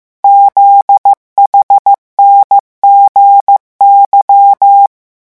Morse code, heard as a keyed tone, about 15 wpm, 800 Hz. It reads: ZHNGY
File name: morse mp3 / revision11 lettres vitesse 14